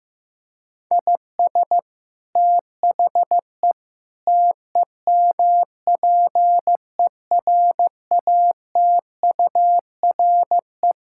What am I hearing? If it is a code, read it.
Morse code, heard as a keyed tone, about 15 wpm, 700 Hz. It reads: IS THE TEMPERATURE
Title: Sentences from Top 1000 Words - Review Part 6 - 15wpm